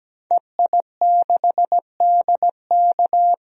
Morse code, heard as a keyed tone, about 17 wpm, 700 Hz. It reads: EI6DK
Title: DXCC-100 Call Signs - 5 Characters - 17wpm